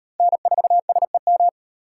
Morse code, heard as a keyed tone, about 38 wpm, 700 Hz. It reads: N4SEM